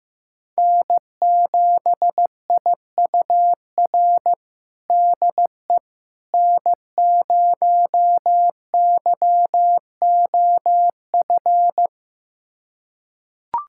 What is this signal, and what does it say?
Morse code, heard as a keyed tone, about 15 wpm, 700 Hz. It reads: N7IUR DE N0YOF
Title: USA Callsign DE Callsign - 5 Characters - 15wpm